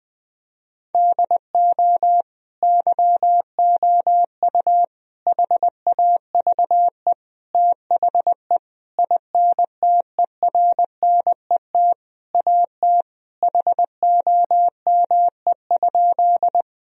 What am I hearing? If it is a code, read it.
Morse code, heard as a keyed tone, about 20 wpm, 700 Hz. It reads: DO YOU HAVE THE INTERNET AT HOME?
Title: Sentences from Top 1000 Words - Review Part 4 - 20wpm